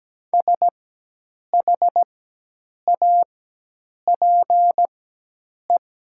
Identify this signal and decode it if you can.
Morse code, heard as a keyed tone, about 17 wpm, spaced as for 8 wpm, 700 Hz. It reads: SHAPE
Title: 4x Character Spacing - Top 500 Words - 17wpm